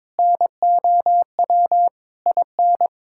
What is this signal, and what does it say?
Morse code, heard as a keyed tone, about 21 wpm, 700 Hz. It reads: NOW IN
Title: Sentences from Top 1000 Words - Review Part 6 - 22wpm